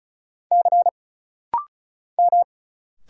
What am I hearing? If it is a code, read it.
Morse code, heard as a keyed tone, about 33 wpm, 700 Hz. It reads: C M
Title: Single Letter-Number Alt - Rapid-Fire 35wpm